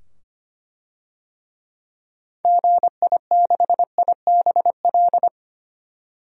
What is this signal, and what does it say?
Morse code, heard as a keyed tone, about 25 wpm, 700 Hz. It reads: GI6IBL
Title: DXCC-100 Call Signs - 6 Characters - 25wpm